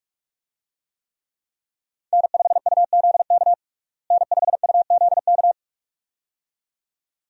Morse code, heard as a keyed tone, about 46 wpm, 700 Hz. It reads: N5UZK N5UZK
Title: USA Call Signs - 5 Characters - Repeated 2x - 45wpm